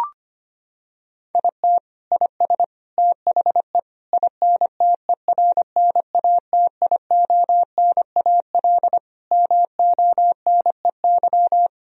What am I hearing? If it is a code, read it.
Morse code, heard as a keyed tone, about 25 wpm, 700 Hz. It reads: IT IS THE INTERNATIONAL MONEY
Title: Sentences from Top 1000 Words - Review Part 4 - 25wpm